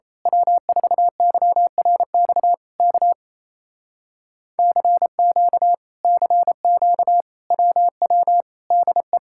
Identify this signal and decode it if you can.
Morse code, heard as a keyed tone, about 26 wpm, 700 Hz. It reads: W4YRX K CQ CQ WW DE
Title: Speed-Racing Common QSO Phrases - 28wpm